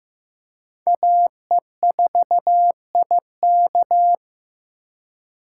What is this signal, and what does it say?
Morse code, heard as a keyed tone, about 15 wpm, 700 Hz. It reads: AE4IK
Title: USA Call Signs - 5 Characters - 3x Character Spacing - 15wpm